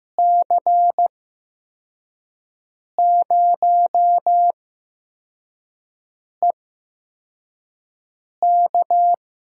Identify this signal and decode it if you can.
Morse code, heard as a keyed tone, about 16 wpm, spaced as for 4 wpm, 700 Hz. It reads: C0EK
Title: USA Call Signs - 6 Characters - 8x Character Spacing - 15wpm